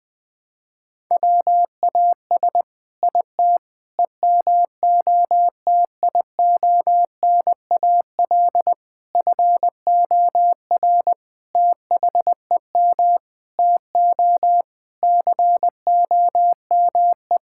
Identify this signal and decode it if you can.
Morse code, heard as a keyed tone, about 20 wpm, 700 Hz. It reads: WAS IT EMOTIONAL FOR THEM TO COME
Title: Sentences from Top 1000 Words - Review Part 5 - 20wpm